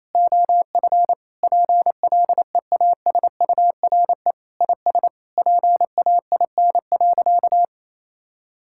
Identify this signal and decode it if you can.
Morse code, heard as a keyed tone, about 27 wpm, 700 Hz. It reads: OF PLEASURE IS PAIN.
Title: Sentences from Top 1600 Words 28wpm